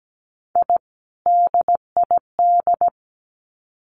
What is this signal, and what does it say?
Morse code, heard as a keyed tone, about 17 wpm, 700 Hz. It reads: I DID
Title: Sets of 2 Words 17wpm